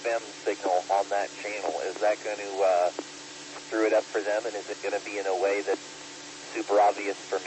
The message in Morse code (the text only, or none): none